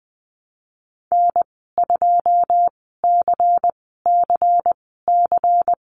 Morse code, heard as a keyed tone, about 20 wpm, spaced as for 14 wpm, 700 Hz. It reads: N2CCC
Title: 154 Lesson 25g - Letter p - Call Signs 20wpm